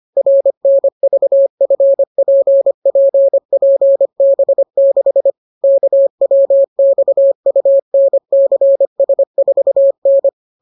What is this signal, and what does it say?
Morse code, heard as a keyed tone, about 25 wpm, 550 Hz. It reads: RNVFPPPB6 KWXUNCS4N